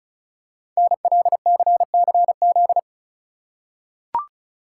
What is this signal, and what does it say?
Morse code, heard as a keyed tone, about 36 wpm, 690 Hz. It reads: NLCCZ
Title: Code Groups - 5 Letters - 35wpm